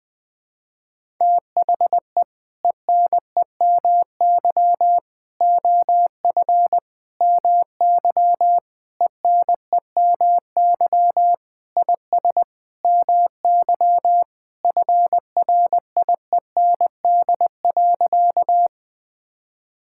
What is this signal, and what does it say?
Morse code, heard as a keyed tone, about 19 wpm, 700 Hz. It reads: THE ENEMY OF MY ENEMY IS MY FRIEND.
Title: Sentences with 8-Word Phrases - Spoken Beginning - 20wpm